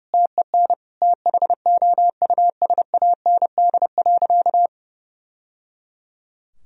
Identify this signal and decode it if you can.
Morse code, heard as a keyed tone, about 30 wpm, 700 Hz. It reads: TEN THOUSAND.
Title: Sentences from Top 1000 Words - Review Part 3 - 30wpm